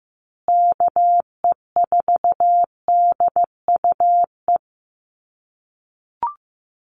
Morse code, heard as a keyed tone, about 15 wpm, 700 Hz. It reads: KE4DUE
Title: USA Call Signs Repeated 2x 15wpm